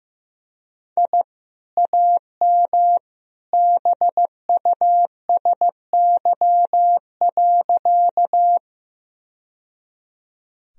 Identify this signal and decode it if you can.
Morse code, heard as a keyed tone, about 15 wpm, 700 Hz. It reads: I AM BUSY.
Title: Sentences from Top 1000 Words - Review Part 3 - 15wpm